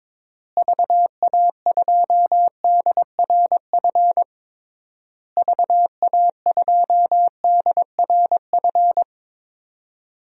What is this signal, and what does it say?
Morse code, heard as a keyed tone, about 22 wpm, 700 Hz. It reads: VA2DRF VA2DRF
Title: DXCC-100 Call Signs - 6 Characters - Repeated 3x - 22wpm